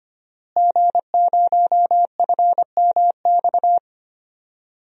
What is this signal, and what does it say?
Morse code, heard as a keyed tone, about 25 wpm, 700 Hz. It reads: G0FMX
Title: DXCC-100 Call Signs - 5 Characters - Repeated 2x - 25wpm